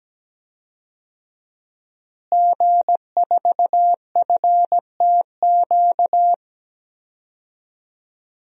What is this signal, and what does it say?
Morse code, heard as a keyed tone, about 17 wpm, 700 Hz. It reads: G4FTQ
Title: DXCC-100 Call Signs - 5 Characters - Repeated 2x - 17wpm